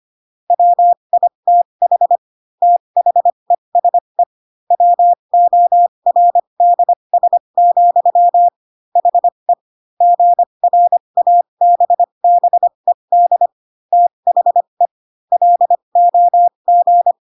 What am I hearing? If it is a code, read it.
Morse code, heard as a keyed tone, about 25 wpm, spaced as for 22 wpm, 700 Hz. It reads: WITH THESE WORDS, HE GRABBED THE LOG